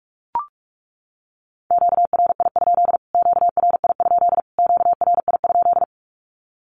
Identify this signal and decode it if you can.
Morse code, heard as a keyed tone, about 44 wpm, 700 Hz. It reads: QRI? QRI? QRI?
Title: Q-Codes Repeated 3x 45wpm